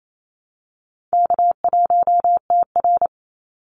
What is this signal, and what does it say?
Morse code, heard as a keyed tone, about 28 wpm, 700 Hz. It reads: K1TR
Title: Easy Call Signs Repeated 3x - 28wpm